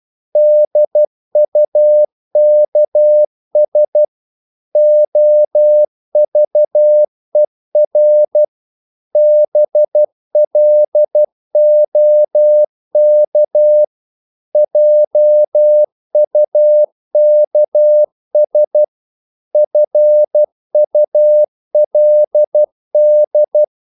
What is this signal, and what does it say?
Morse code, heard as a keyed tone, about 12 wpm, 600 Hz. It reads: DUKS OVER BLOK JUKS FULD
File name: Korte ord DK 12wpm